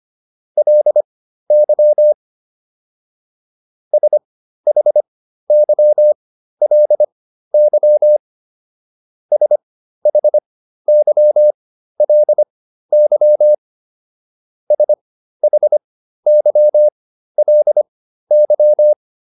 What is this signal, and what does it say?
Morse code, heard as a keyed tone, about 25 wpm, spaced as for 13 wpm, 600 Hz. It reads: LY SHYLY SHYLY SHYLY